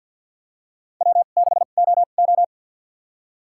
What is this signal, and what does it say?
Morse code, heard as a keyed tone, about 48 wpm, spaced as for 35 wpm, 700 Hz. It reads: W6OO
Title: USA Call Signs - 4 Characters - 2x Character Spacing - 50wpm